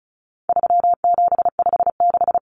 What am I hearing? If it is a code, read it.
Morse code, heard as a keyed tone, about 34 wpm, 700 Hz. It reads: 3756
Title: Common Cut Numbers - 4-Digit 35wpm